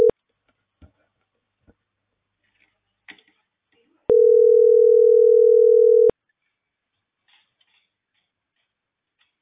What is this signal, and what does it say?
Ringback tone (North America), 440 + 480 Hz.